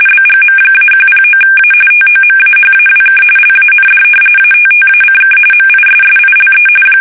RTTY, about 45.45 baud. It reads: BGM-109A TOMAHAWK LAND ATTACK MISSILE
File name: fsk850 / battleshipnj